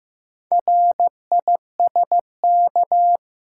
Morse code, heard as a keyed tone, about 15 wpm, 700 Hz. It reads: RISK
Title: Sentences from Top 600 Words - New Word Drill - 15wpm